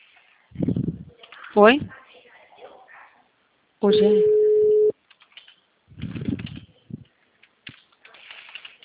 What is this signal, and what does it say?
Ringback tone (Europe, 425 Hz), with speech.
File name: out- -2023- - - .53706